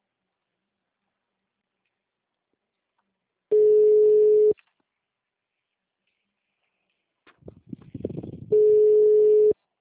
Ringback tone (Europe, 425 Hz).
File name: out- -2002- - - .190